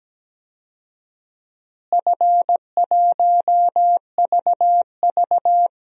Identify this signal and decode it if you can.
Morse code, heard as a keyed tone, about 17 wpm, 700 Hz. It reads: F1VV
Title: DXCC-100 Call Signs - 4 Characters - Repeated 3x - 17wpm